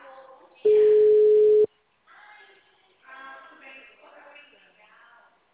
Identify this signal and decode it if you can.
Ringback tone (Europe, 425 Hz), with speech.